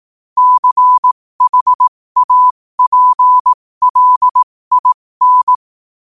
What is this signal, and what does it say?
Morse code, heard as a keyed tone, about 18 wpm, spaced as for 15 wpm, 1000 Hz. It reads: CHAPLIN
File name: qotdfiles / 15wpm